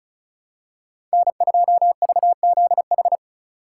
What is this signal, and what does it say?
Morse code, heard as a keyed tone, about 34 wpm, 700 Hz. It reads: N2VZH
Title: USA Call Signs - 5 Characters - 4x Character Spacing - 35wpm